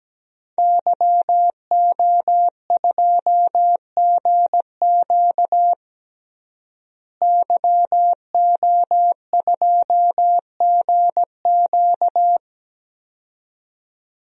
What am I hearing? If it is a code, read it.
Morse code, heard as a keyed tone, about 17 wpm, 700 Hz. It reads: YO2GQ YO2GQ
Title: DXCC-100 Call Signs - 5 Characters - Repeated 2x - 17wpm